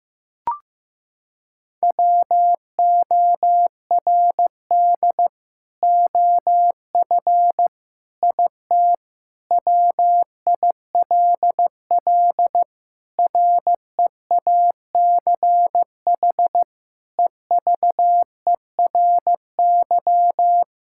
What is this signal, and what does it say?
Morse code, heard as a keyed tone, about 15 wpm, 700 Hz. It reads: WORD OF IT WILL REACH EVERY
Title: Sentences from Top 1600 Words 15wpm